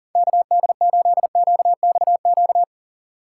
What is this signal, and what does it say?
Morse code, heard as a keyed tone, about 37 wpm, 700 Hz. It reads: KD8QXQ
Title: USA Call Signs - 6x Character Spacing - 40wpm